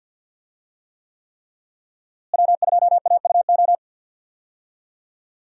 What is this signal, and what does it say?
Morse code, heard as a keyed tone, about 53 wpm, 700 Hz. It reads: W2AUO
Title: USA Call Signs - 5 Characters - Repeated 3x - 50wpm